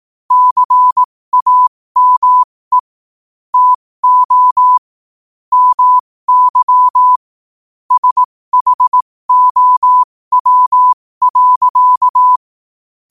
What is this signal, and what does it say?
Morse code, heard as a keyed tone, about 18 wpm, spaced as for 15 wpm, 1000 Hz. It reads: CAME TO MY SHOW.
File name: qotdfiles / 15wpm